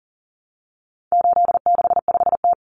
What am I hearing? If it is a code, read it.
Morse code, heard as a keyed tone, about 40 wpm, 700 Hz. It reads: 865T
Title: Common Cut Numbers - 4-Digit 40wpm